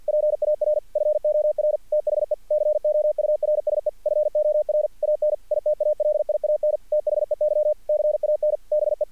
Morse code, heard as a keyed tone, about 46 wpm, 600 Hz, with radio noise. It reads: PAR FOR THE COURSE FOR AN ITALIAN THEY CAN BE
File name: archived / CRSnet